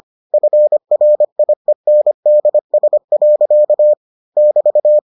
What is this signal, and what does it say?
Morse code, heard as a keyed tone, about 25 wpm, 600 Hz. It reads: FRIENDS. <BT>